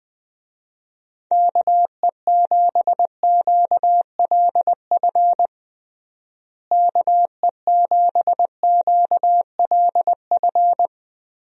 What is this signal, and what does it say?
Morse code, heard as a keyed tone, about 20 wpm, 700 Hz. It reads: KE7QLF KE7QLF
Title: USA Call Signs - 6 Characters - Repeated 2x - 20wpm